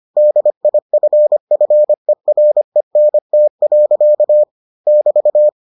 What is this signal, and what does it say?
Morse code, heard as a keyed tone, about 25 wpm, 600 Hz. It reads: DIFFERENT. <BT>